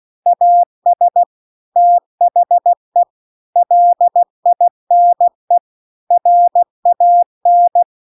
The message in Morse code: AS THE LINE RAN

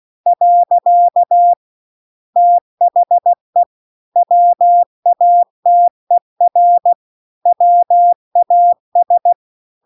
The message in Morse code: . THE WATER WAS

